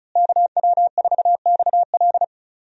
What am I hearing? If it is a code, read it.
Morse code, heard as a keyed tone, about 36 wpm, 700 Hz. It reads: KW4XL
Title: USA Call Signs - 5 Characters - Repeated 3x - 35wpm